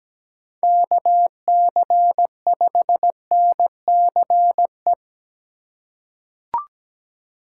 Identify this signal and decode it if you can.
Morse code, heard as a keyed tone, about 17 wpm, 700 Hz. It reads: KC5NCE
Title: USA Call Signs - 6 Characters - 2x Character Spacing - 17wpm